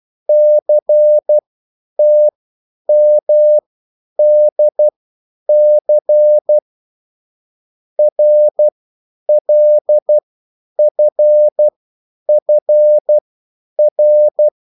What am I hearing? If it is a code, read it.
Morse code, heard as a keyed tone, about 12 wpm, spaced as for 9 wpm, 600 Hz. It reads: CTMDC RLFFR